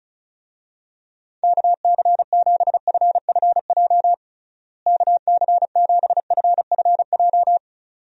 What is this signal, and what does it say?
Morse code, heard as a keyed tone, about 36 wpm, 700 Hz. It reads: KC7FFJ KC7FFJ
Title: USA Call Signs - 6 Characters - Repeated 2x - 35wpm